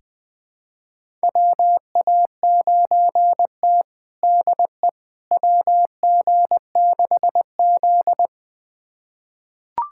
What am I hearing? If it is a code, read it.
Morse code, heard as a keyed tone, about 20 wpm, 700 Hz. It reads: WA9T DE WG6Z
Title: USA Callsign DE Callsign - 4 Characters - 20wpm